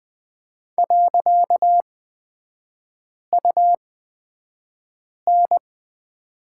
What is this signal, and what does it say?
Morse code, heard as a keyed tone, about 20 wpm, spaced as for 5 wpm, 700 Hz. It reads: .UN
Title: 255 Lesson 41e - Period - Three Characters 20wpm